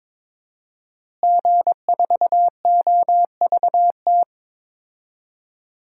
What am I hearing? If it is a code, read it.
Morse code, heard as a keyed tone, about 22 wpm, 700 Hz. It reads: G4OVT
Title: DXCC-100 Call Signs - 5 Characters - Repeated 3x - 22wpm